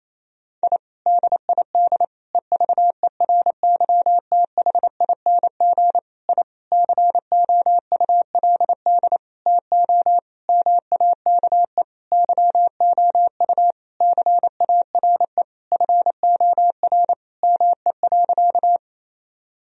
Morse code, heard as a keyed tone, about 28 wpm, 700 Hz. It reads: I DID EVERYTHING I COULD TO MAKE YOU CARE FOR ME.
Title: Sentences from Top 1000 Words - Review Part 1 - 28wpm